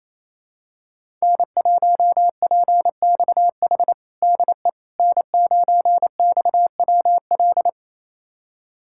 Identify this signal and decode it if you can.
Morse code, heard as a keyed tone, about 27 wpm, 700 Hz. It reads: N1PXH DE N9XWL
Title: USA Callsign DE Callsign - 5 Characters - 28wpm